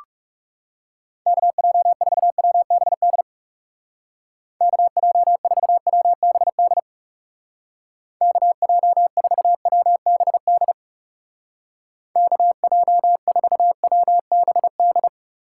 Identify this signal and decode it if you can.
Morse code, heard as a keyed tone, about 37 wpm, 700 Hz. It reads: KJ4WBD KJ4WBD KJ4WBD KJ4WBD
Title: Speed Racing - USA 2x3 Call Signs 30wpm